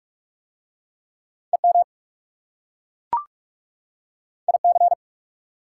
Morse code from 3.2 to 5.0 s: IC

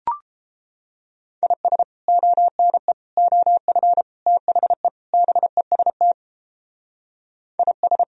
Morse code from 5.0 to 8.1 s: BEST IS

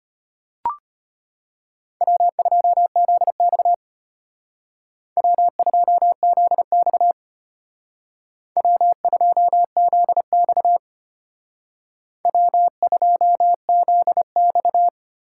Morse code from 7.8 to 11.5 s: W2ZX

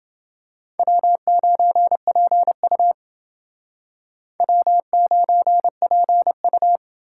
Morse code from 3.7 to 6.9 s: W9PU

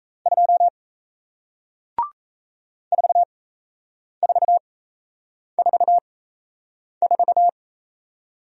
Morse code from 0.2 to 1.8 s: 2

